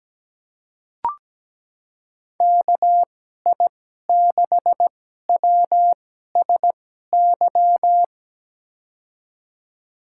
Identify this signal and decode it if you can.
Morse code, heard as a keyed tone, about 17 wpm, spaced as for 12 wpm, 700 Hz. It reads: KI6WSY
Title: USA Call Signs - 6 Characters - 2x Character Spacing - 17wpm